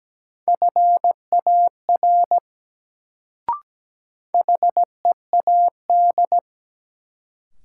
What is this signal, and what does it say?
Morse code, heard as a keyed tone, about 17 wpm, 700 Hz. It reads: FAR HEAD
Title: Sentences from Top 300 Words - New Word Drill - 17wpm